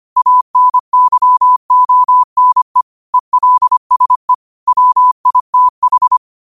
Morse code, heard as a keyed tone, about 25 wpm, 1000 Hz. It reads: ANYONE ELSE WITH